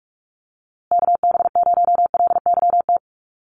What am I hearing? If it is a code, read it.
Morse code, heard as a keyed tone, about 44 wpm, 700 Hz. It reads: KB0LYT
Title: USA Call Signs Repeated 3x 45wpm